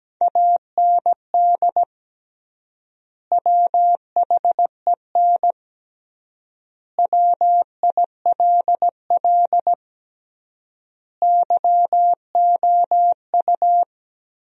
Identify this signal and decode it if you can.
Morse code, heard as a keyed tone, about 17 wpm, 700 Hz. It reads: AND WHEN WILL YOU
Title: 3x Word-Spacing - Sentences from Top 100 Words - 17wpm